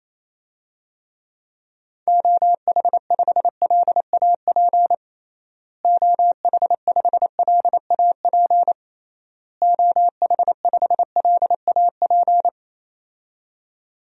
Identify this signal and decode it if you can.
Morse code, heard as a keyed tone, about 28 wpm, 700 Hz. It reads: OH5LAP OH5LAP OH5LAP
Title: DXCC-100 Call Signs - 6 Characters - Repeated 3x - 28wpm